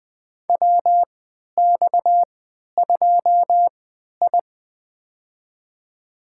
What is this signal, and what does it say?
Morse code, heard as a keyed tone, about 20 wpm, spaced as for 11 wpm, 700 Hz. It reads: WX2I
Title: USA Call Signs - 4 Characters - 3x Character Spacing - 20wpm